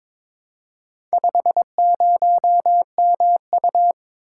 Morse code, heard as a keyed tone, about 22 wpm, 700 Hz. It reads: 50MU
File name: practice / 22wpm / Four Letter-Number Combinations - 22wpm